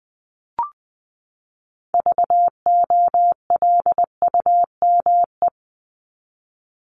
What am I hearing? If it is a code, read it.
Morse code, heard as a keyed tone, about 20 wpm, 700 Hz. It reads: VOLUME
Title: Top 1751-2000 Words 20wpm